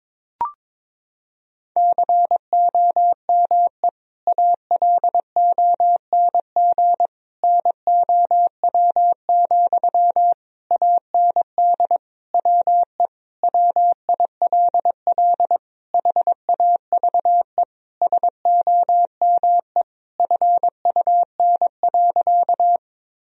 Morse code, heard as a keyed tone, about 22 wpm, 700 Hz. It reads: COME ALONG NOW, AND WE WILL HAVE SOME FUN.
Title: Sentences from Top 1000 Words - Review Part 4 - 22wpm